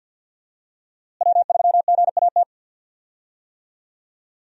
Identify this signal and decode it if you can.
Morse code, heard as a keyed tone, about 48 wpm, 700 Hz. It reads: W3GAT
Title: USA Call Signs - 5 Characters - Repeated 3x - 50wpm